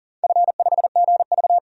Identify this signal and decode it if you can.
Morse code, heard as a keyed tone, about 37 wpm, 700 Hz. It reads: F5GV
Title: DXCC-100 Call Signs - 4 Characters - 40wpm